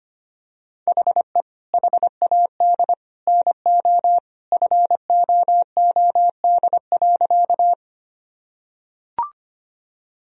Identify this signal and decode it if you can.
Morse code, heard as a keyed tone, about 25 wpm, 700 Hz. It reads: HE HAD NO FOOD.
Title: Sentences from Top 1000 Words - Review Part 1 - 25wpm